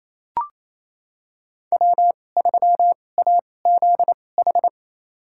Morse code, heard as a keyed tone, about 28 wpm, spaced as for 20 wpm, 700 Hz. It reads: W3AZH
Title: USA Call Signs - 2x Character Spacing - 28wpm